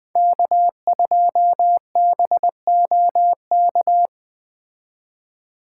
Morse code, heard as a keyed tone, about 20 wpm, 700 Hz. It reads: K2BOK
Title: Speed Racing - USA Call Signs 17wpm